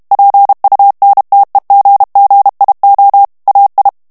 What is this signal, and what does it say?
Morse code, heard as a keyed tone, about 32 wpm, 800 Hz. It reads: PUNTEGGIO AI
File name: ITALIEN texte clair 2 vitesse 30 mots minute